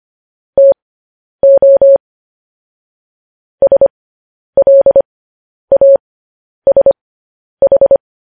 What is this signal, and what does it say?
Morse code, heard as a keyed tone, about 25 wpm, spaced as for 10 wpm, 550 Hz. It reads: TO SLASH
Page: news-headlines-in-morse-code-at-10-wpm